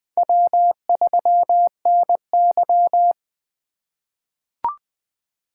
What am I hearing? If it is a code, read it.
Morse code, heard as a keyed tone, about 21 wpm, 700 Hz. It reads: W3NY
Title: 2023 POTA Activator Call Signs - 20wpm